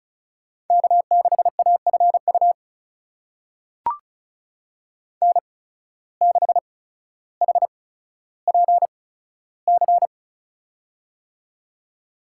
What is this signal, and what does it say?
Morse code, heard as a keyed tone, about 34 wpm, spaced as for 15 wpm, 700 Hz. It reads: K6AFU N6HPC
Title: USA Call Signs - 8x Character Spacing - 35wpm